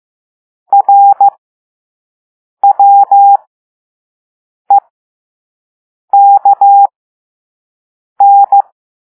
Morse code, heard as a keyed tone, about 15 wpm, spaced as for 5 wpm, 800 Hz. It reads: RWEKN
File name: Morse / Level5-All-B